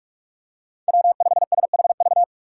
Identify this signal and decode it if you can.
Morse code, heard as a keyed tone, about 42 wpm, 690 Hz. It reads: W5SHV
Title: USA Call Signs - 5 Characters - 2x Character Spacing - 45wpm